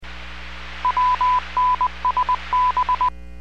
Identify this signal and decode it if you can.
Morse code, heard as a keyed tone, about 20 wpm, 1000 Hz. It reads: WNSB